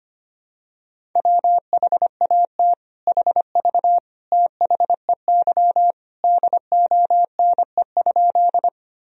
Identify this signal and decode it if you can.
Morse code, heard as a keyed tone, about 25 wpm, 700 Hz. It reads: WHAT HV THEY DONE?
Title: Sentences from Top 500 Words and Abbreviations 25wpm